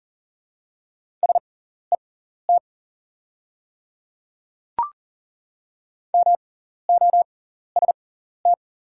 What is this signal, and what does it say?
Morse code, heard as a keyed tone, about 40 wpm, spaced as for 14 wpm, 700 Hz. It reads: SET MOST
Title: 6x Character Spacing - Top 300 Words - 40wpm